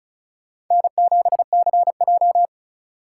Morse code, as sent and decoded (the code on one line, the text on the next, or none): -. --... -.-. .---
N7CJ